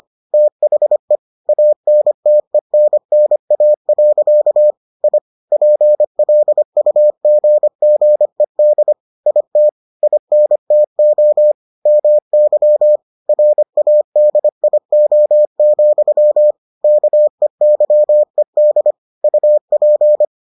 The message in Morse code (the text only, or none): THE ANTENNA. I PLUGGED IT INTO MY RADIO, KEYED UP